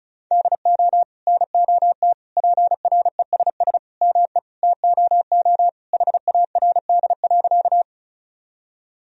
Morse code, heard as a keyed tone, about 36 wpm, 700 Hz. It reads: DO NOT PRESS ME TOO HARD.
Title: Sentences from Top 1000 Words - Review Part 1 - 35wpm